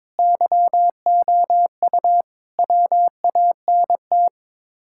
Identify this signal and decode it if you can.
Morse code, heard as a keyed tone, about 22 wpm, 700 Hz. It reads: YOU WANT
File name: practice / 22wpm / Sets of 7 Words - 22wpm